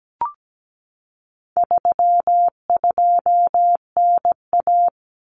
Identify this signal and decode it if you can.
Morse code, heard as a keyed tone, about 17 wpm, 700 Hz. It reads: 32NA
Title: Common Cut Numbers - 4-Digit 17wpm